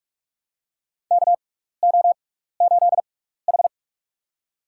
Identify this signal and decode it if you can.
Morse code, heard as a keyed tone, about 46 wpm, spaced as for 15 wpm, 700 Hz. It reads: KO8H
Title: USA Call Signs - 4 Characters - 6x Character Spacing - 45wpm